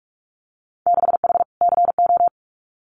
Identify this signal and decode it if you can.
Morse code, heard as a keyed tone, about 46 wpm, 700 Hz. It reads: 6H CO
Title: Winter Field Day Exchanges Encore 45wpm